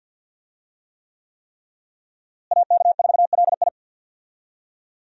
Morse code, heard as a keyed tone, about 46 wpm, 700 Hz. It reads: AK4LI
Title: USA Call Signs - 5 Characters - Repeated 3x - 50wpm